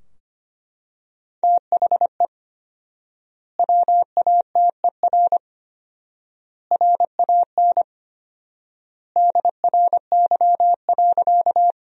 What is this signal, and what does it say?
Morse code, heard as a keyed tone, about 25 wpm, 700 Hz. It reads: THE WATER RAN DRY.
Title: 4x Word-Spacing - Sentences from Top 500 Words - 25wpm